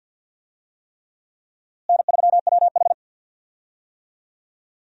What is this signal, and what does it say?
Morse code, heard as a keyed tone, about 50 wpm, 700 Hz. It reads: N3WH
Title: USA Call Signs - 4 Characters - Repeated 3x - 50wpm